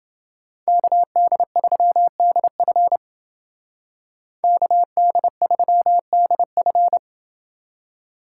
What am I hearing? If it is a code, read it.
Morse code, heard as a keyed tone, about 30 wpm, 700 Hz. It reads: KD3DF KD3DF